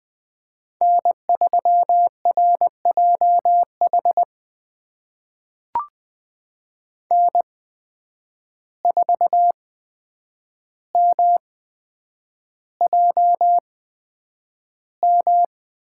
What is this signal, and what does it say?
Morse code, heard as a keyed tone, about 20 wpm, spaced as for 9 wpm, 700 Hz. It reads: N3RJH N4MJM
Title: USA Call Signs - 5 Characters - 8x Character Spacing - 20wpm